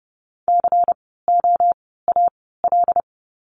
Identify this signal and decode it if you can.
Morse code, heard as a keyed tone, about 30 wpm, spaced as for 17 wpm, 700 Hz. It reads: COAL
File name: Lesson 16f - Letter c - Words 30wpm